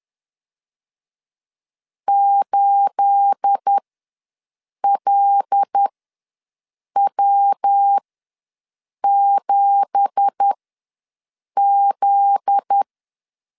Morse code, heard as a keyed tone, about 10 wpm, spaced as for 6 wpm, 800 Hz. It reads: 8LW7Z
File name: LEZIONI / 9 2867 ws 10wpm 4215